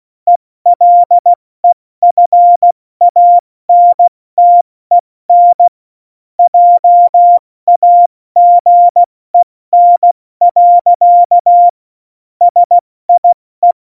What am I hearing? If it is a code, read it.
Morse code, heard as a keyed tone, about 16 wpm, 700 Hz. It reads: ELEFANTEN JAGEN. SIE
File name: Morsekurs / Witze / Witz 03.Juli 201, 70 F80